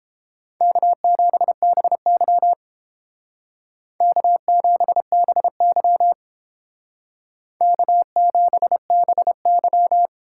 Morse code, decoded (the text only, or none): K7BY K7BY K7BY